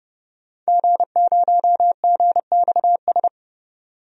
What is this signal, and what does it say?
Morse code, heard as a keyed tone, about 29 wpm, 700 Hz. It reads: G0GXS
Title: DXCC-100 Call Signs - 5 Characters - 30wpm